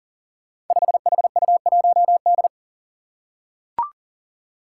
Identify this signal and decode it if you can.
Morse code, heard as a keyed tone, about 39 wpm, 700 Hz. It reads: 5HU1D
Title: Five Challenging Letter-Number Combinations 40wpm